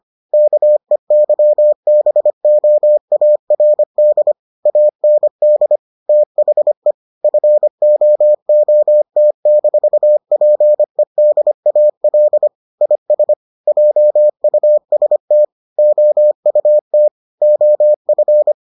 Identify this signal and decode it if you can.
Morse code, heard as a keyed tone, about 25 wpm, 600 Hz. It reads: KEYBOARD AND THE FOOT-PEDAL IS JUST OUT OF